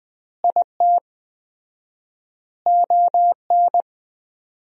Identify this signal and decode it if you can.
Morse code, heard as a keyed tone, about 20 wpm, 700 Hz. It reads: IT ON
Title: 4x Word-Spacing - Sentences from Top 400 Words - 20wpm